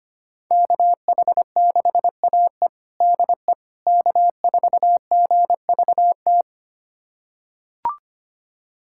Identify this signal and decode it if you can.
Morse code, heard as a keyed tone, about 25 wpm, 700 Hz. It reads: KH6AE DE K4GVT
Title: USA Callsign DE Callsign - 5 Characters - 25wpm